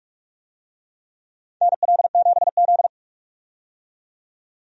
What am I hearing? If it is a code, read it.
Morse code, heard as a keyed tone, about 46 wpm, 700 Hz. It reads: NL7Z
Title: USA Call Signs - 4 Characters - 45wpm